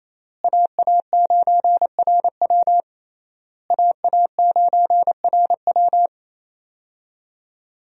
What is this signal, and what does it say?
Morse code, heard as a keyed tone, about 28 wpm, 700 Hz. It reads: AA9RW AA9RW
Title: USA Call Signs - 5 Characters - Repeated 3x - 28wpm